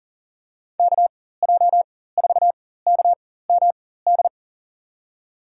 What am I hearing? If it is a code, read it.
Morse code, heard as a keyed tone, about 40 wpm, spaced as for 19 wpm, 700 Hz. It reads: KJ4KMD
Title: USA Call Signs - 6 Characters - 4x Character Spacing - 40wpm